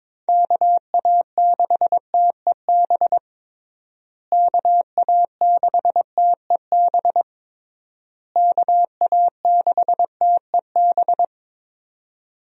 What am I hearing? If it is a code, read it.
Morse code, heard as a keyed tone, about 22 wpm, 700 Hz. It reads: KA6TEB KA6TEB KA6TEB
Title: USA Call Signs - 6 Characters - Repeated 3x - 22wpm